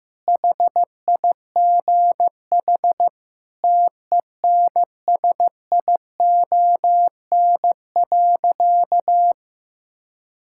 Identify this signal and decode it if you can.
Morse code, heard as a keyed tone, about 16 wpm, 700 Hz. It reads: HIGH TENSION.
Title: Sentences from Top 1600 Words 15wpm